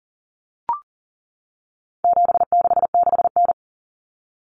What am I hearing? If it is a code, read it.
Morse code, heard as a keyed tone, about 40 wpm, 700 Hz. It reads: 766N
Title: Common Cut Numbers - 4-Digit 40wpm